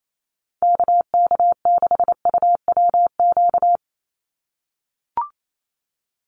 Morse code, heard as a keyed tone, about 28 wpm, 700 Hz. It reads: KK6UWQ
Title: USA Call Signs 28wpm